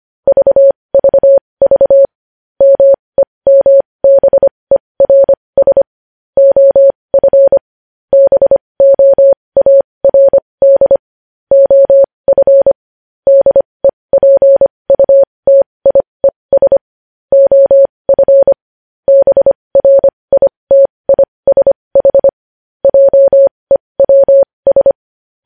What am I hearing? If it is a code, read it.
Morse code, heard as a keyed tone, about 25 wpm, spaced as for 20 wpm, 550 Hz. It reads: VVV MEMBERS OF BOARD OF DEPUTIES OF BRITISH JEWS
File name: podcast / News headlines-1x-25x20